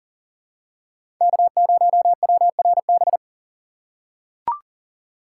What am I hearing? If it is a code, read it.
Morse code, heard as a keyed tone, about 37 wpm, 700 Hz. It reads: K0WRB
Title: USA Call Signs - 2x Character Spacing - 40wpm